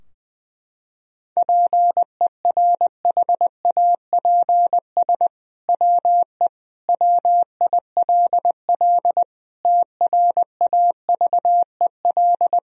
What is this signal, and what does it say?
Morse code, heard as a keyed tone, about 20 wpm, 700 Hz. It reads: PERHAPS WE WILL TRAVEL
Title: Sentences from Top 1000 Words - Review Part 2 - 20wpm